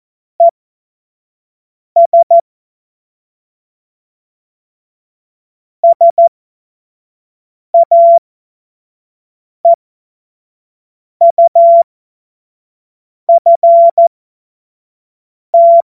Morse code, heard as a keyed tone, about 14 wpm, spaced as for 5 wpm, 680 Hz. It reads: ES SAEUFT
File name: 25-F70